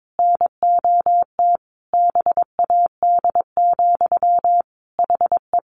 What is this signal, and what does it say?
Morse code, heard as a keyed tone, about 22 wpm, 700 Hz. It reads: NOT BAD, HE
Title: Sentences from Top 700 Words 22wpm